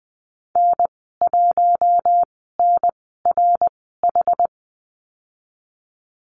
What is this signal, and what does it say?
Morse code, heard as a keyed tone, about 20 wpm, spaced as for 15 wpm, 700 Hz. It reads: N1NRH
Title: 088 Lesson 15g - Number 5 - Call Signs 20wpm